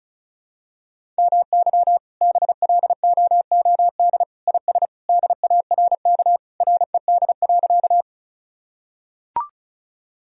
Morse code, heard as a keyed tone, about 36 wpm, 700 Hz. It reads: MY BLOOD IS DARK RED.
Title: Sentences from Top 1000 Words - Review Part 2 - 35wpm